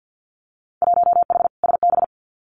Morse code, heard as a keyed tone, about 50 wpm, 700 Hz. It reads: 1H SB
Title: Winter Field Day Exchanges 50wpm